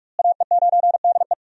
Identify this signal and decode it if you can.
Morse code, heard as a keyed tone, about 42 wpm, 700 Hz. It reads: AE9DE
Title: USA Call Signs - 5 Characters - Repeated 2x - 45wpm